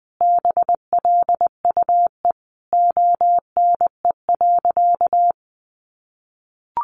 Morse code, heard as a keyed tone, about 20 wpm, 700 Hz. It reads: BLUE ONE.
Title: Additional Easy Phrases 20wpm